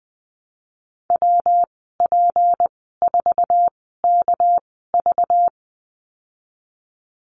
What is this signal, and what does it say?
Morse code, heard as a keyed tone, about 20 wpm, spaced as for 15 wpm, 700 Hz. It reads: WP4KV
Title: 213 Lesson 34g - Number 8 - Call Signs 20wpm